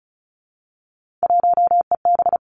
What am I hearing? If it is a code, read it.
Morse code, heard as a keyed tone, about 34 wpm, 700 Hz. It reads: 1EB
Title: Three Challenging Letter-Number Combinations 35wpm